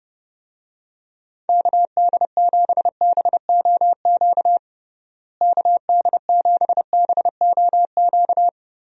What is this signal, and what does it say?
Morse code, heard as a keyed tone, about 30 wpm, 700 Hz. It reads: KD7BOQ KD7BOQ
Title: USA Call Signs - 6 Characters - Repeated 3x - 30wpm